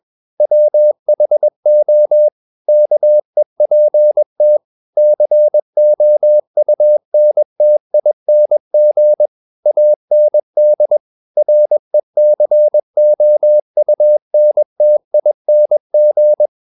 Morse code, heard as a keyed tone, about 21 wpm, 600 Hz. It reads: WHO KEPT COUNTING AND RECOUNTING
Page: CW med Gnister